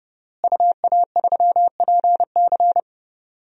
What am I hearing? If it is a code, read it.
Morse code, heard as a keyed tone, about 30 wpm, 700 Hz. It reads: UA3PC
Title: DXCC-100 Call Signs - 5 Characters - Repeated 3x - 30wpm